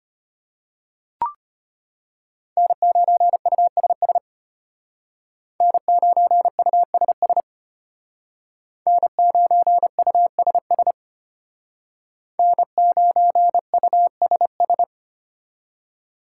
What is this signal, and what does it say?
Morse code, heard as a keyed tone, about 33 wpm, 700 Hz. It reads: N9USS N9USS N9USS N9USS